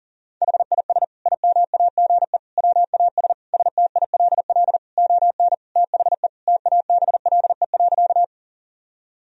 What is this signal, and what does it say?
Morse code, heard as a keyed tone, about 40 wpm, 700 Hz. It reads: HIS IMAGE WAS STILL ON THE TABLE.
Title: Sentences from Top 1000 Words - Review Part 4 - 40wpm